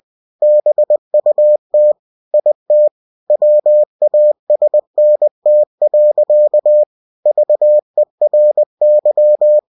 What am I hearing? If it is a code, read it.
Morse code, heard as a keyed tone, about 20 wpm, 600 Hz. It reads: BUT IT WASNT. VERY
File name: cw-books / winnie-the-pooh / winnie-20wpm-page-0075